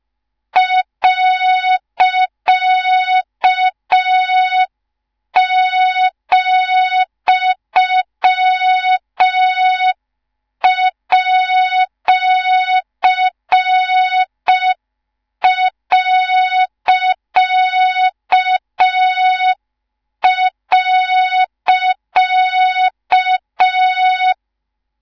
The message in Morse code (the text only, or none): .,@..